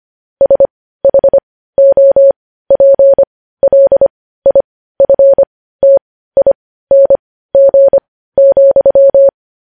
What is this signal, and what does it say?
Morse code, heard as a keyed tone, about 25 wpm, spaced as for 15 wpm, 550 Hz. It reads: SHOPLIFTING,
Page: news-headlines-in-morse-code-at-15-wpm